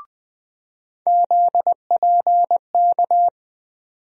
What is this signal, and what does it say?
Morse code, heard as a keyed tone, about 20 wpm, 700 Hz. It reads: ZPK
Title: USA Call Sign Postfixes - 3 Letters 20wpm